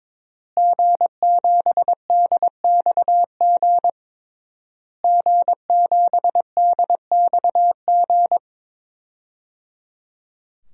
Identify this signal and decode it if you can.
Morse code, heard as a keyed tone, about 22 wpm, 700 Hz. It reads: G7DXG G7DXG
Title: DXCC-100 Call Signs - 5 Characters - Repeated 3x - 22wpm